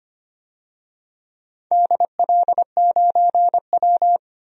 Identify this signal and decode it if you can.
Morse code, heard as a keyed tone, about 25 wpm, 700 Hz. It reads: DL9W
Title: DXCC-100 Call Signs - 4 Characters - 25wpm